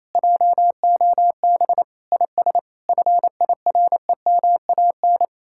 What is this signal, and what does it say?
Morse code, heard as a keyed tone, about 35 wpm, 700 Hz. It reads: JOB IS FIREMAN